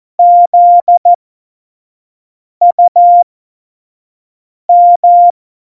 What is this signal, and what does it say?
Morse code, heard as a keyed tone, about 14 wpm, spaced as for 5 wpm, 700 Hz. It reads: ZUM
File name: Morsekurs / Witze / Witz 14. November 2017, 25 F70